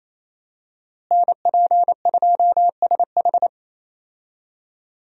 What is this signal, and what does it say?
Morse code, heard as a keyed tone, about 28 wpm, 700 Hz. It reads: NP2SH
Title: USA Call Signs - 5 Characters - Repeated 2x - 28wpm